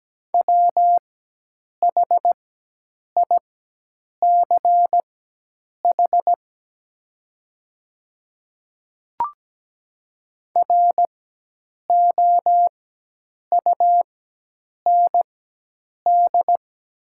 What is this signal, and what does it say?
Morse code, heard as a keyed tone, about 17 wpm, spaced as for 8 wpm, 700 Hz. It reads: WHICH ROUND
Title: 4x Character Spacing - Top 300 Words - 17wpm